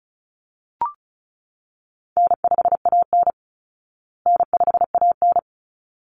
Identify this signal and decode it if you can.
Morse code, heard as a keyed tone, about 34 wpm, 700 Hz. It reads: N5AN N5AN
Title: Call Signs Repeated 3x - 35wpm